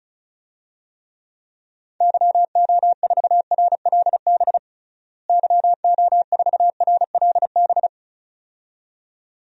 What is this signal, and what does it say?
Morse code, heard as a keyed tone, about 36 wpm, 700 Hz. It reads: YO4RLB YO4RLB
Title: DXCC-100 Call Signs - 6 Characters - Repeated 2x - 35wpm